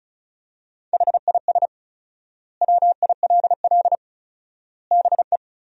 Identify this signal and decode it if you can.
Morse code, heard as a keyed tone, about 34 wpm, 700 Hz. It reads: HIS WILL BE
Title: 4x Word-Spacing - Sentences from Top 500 Words - 35wpm